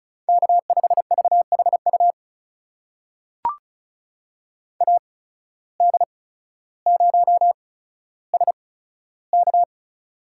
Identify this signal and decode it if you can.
Morse code, heard as a keyed tone, about 36 wpm, spaced as for 15 wpm, 700 Hz. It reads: K5VHU AD0SK
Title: USA Call Signs - 5 Characters - 8x Character Spacing - 35wpm